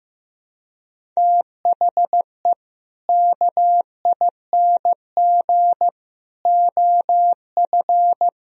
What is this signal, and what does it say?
Morse code, heard as a keyed tone, about 16 wpm, 700 Hz. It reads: THE KING OF
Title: Sentences from Top 1000 Words - Review Part 1 - 15wpm